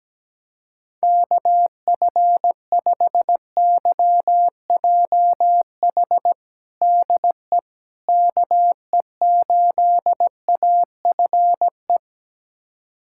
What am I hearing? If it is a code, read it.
Morse code, heard as a keyed tone, about 17 wpm, 700 Hz. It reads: KF5YJH DE KE8AFE
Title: USA Callsign DE Callsign - 6 Characters - 17wpm